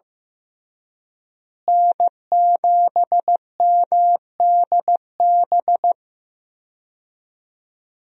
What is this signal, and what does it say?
Morse code, heard as a keyed tone, about 15 wpm, 700 Hz. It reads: N7MDB
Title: USA Call Signs - 5 Characters - Repeated 3x - 15wpm